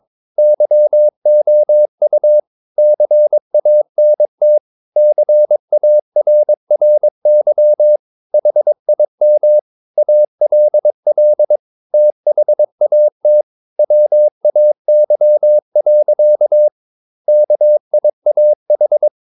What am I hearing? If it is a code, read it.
Morse code, heard as a keyed tone, about 22 wpm, 600 Hz. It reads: YOU CANT CARRY HIM ALL THAT WAY. KIAH
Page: Romanen Never af Ken Follet på engelsk.